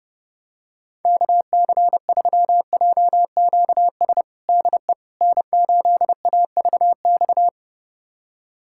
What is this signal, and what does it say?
Morse code, heard as a keyed tone, about 32 wpm, 700 Hz. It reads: KC3JQS DE N8AVX
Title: USA Callsign DE Callsign 30wpm